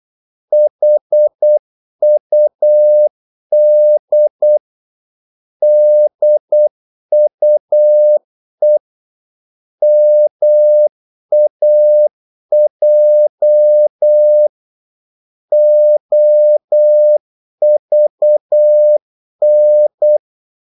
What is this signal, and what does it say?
Morse code, heard as a keyed tone, about 8 wpm, 600 Hz. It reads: HUD DUE MAJ OVN
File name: Korte ord DK 8wpm